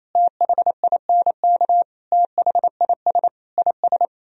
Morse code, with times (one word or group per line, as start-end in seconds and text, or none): 0.1-1.8 THINK
2.1-3.3 THIS
3.6-4.0 IS